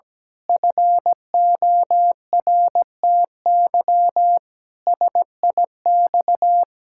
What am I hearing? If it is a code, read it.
Morse code, heard as a keyed tone, about 17 wpm, 700 Hz. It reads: FORTY SIX
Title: Pangram Challenge v2 - 17wpm